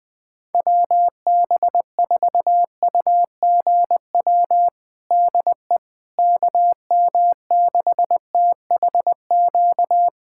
Morse code, heard as a keyed tone, about 20 wpm, 700 Hz. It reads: WB4UGW DE KM6THQ
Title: USA Callsign DE Callsign - 6 Characters - 20wpm